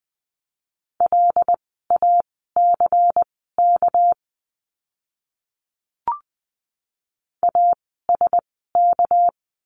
Morse code, heard as a keyed tone, about 20 wpm, spaced as for 15 wpm, 700 Hz. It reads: LACK ASK
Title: 199 Lesson 32f - Letter k - Words 20wpm